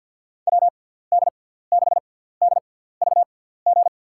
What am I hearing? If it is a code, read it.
Morse code, heard as a keyed tone, about 47 wpm, spaced as for 17 wpm, 700 Hz. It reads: WD6DVG
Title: USA Call Signs - 6 Characters - 6x Character Spacing - 50wpm